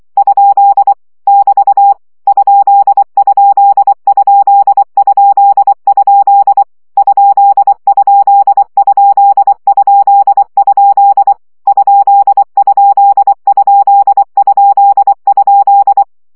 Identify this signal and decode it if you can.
Morse code, heard as a keyed tone, about 24 wpm, 800 Hz. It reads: ? <BT> ????? ????? ?????
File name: CW gyakorlas kerdojel 20 baud